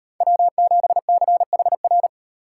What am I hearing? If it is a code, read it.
Morse code, heard as a keyed tone, about 38 wpm, 700 Hz. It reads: W7CHR